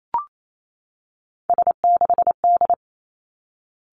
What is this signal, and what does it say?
Morse code, heard as a keyed tone, about 28 wpm, 700 Hz. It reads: S6D
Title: Three Challenging Letter-Number Combinations 28wpm